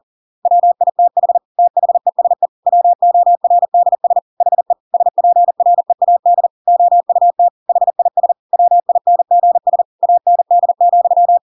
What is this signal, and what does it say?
Morse code, heard as a keyed tone, about 40 wpm, 700 Hz. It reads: WITH THESE WORDS HE SPREAD OUT HIS WINGS AND,
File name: Pinocchio / Pinocchio ch5 40 40 ik1hge